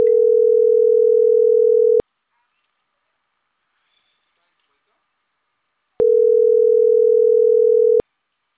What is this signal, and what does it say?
Ringback tone (North America), 440 + 480 Hz.